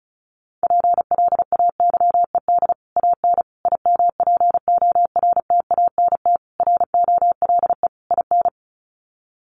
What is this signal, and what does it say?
Morse code, heard as a keyed tone, about 34 wpm, 700 Hz. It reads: PLAYED AN IMPORTANT ROLE IN
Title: Sets of 5 Words Encore2 35wpm